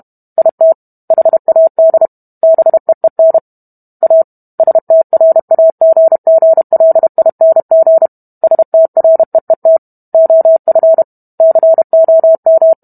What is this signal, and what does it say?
Morse code, heard as a keyed tone, about 32 wpm, 650 Hz. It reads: IT HAD BEEN A STRAGGLING STREET OF COM